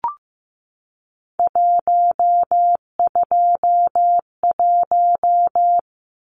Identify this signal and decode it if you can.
Morse code, heard as a keyed tone, about 15 wpm, 700 Hz. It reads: 121
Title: Numbers - 3-Digit 15wpm